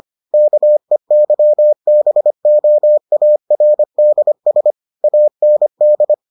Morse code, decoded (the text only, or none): KEYBOARDS AND